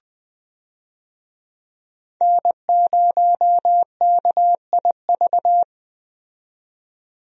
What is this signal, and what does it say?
Morse code, heard as a keyed tone, about 20 wpm, 700 Hz. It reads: N0KIV
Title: USA Call Signs - 5 Characters - 20wpm